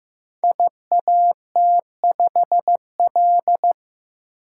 Morse code, heard as a keyed tone, about 15 wpm, 700 Hz. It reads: IAT5L
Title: Code Groups - 5 Letter:Numbers - 15wpm